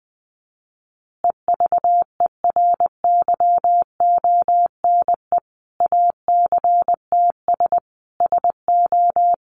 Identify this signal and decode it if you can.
Morse code, heard as a keyed tone, about 20 wpm, 700 Hz. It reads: EVERYONE ACTS SO
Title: Easy Phrases 20wpm